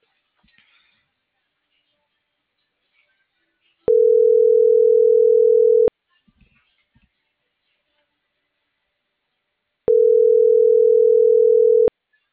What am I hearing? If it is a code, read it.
Ringback tone (North America), 440 + 480 Hz.